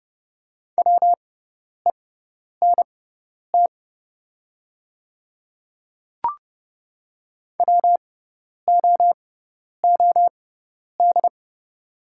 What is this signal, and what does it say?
Morse code, heard as a keyed tone, about 30 wpm, spaced as for 10 wpm, 690 Hz. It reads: WENT WOOD
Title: 6x Character Spacing - Top 300 Words - 30wpm